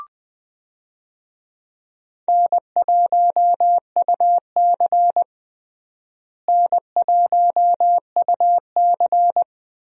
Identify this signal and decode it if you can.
Morse code, heard as a keyed tone, about 20 wpm, 700 Hz. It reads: N1UC N1UC
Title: USA Call Signs - 4 Characters - Repeated 2x - 20wpm